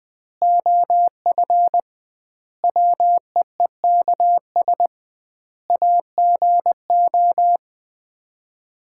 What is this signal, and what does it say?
Morse code, heard as a keyed tone, about 20 wpm, 700 Hz. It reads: OF WEEKS AGO
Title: 2x Word-Spacing - Sets of 5 Words - 20wpm